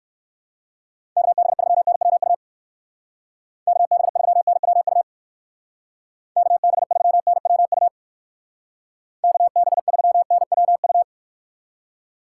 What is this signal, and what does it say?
Morse code, heard as a keyed tone, about 59 wpm, 700 Hz. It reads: KB3NWU KB3NWU KB3NWU KB3NWU